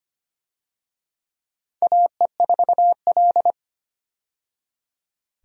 Morse code, heard as a keyed tone, about 25 wpm, 700 Hz. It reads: AE4L
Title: USA Call Signs - 4 Characters - 25wpm